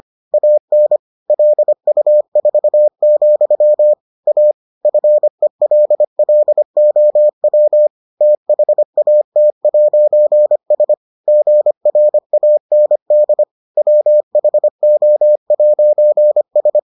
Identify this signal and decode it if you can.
Morse code, heard as a keyed tone, about 25 wpm, 600 Hz. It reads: AN LU4, A FELLOW THAT'S GRAND WHO'S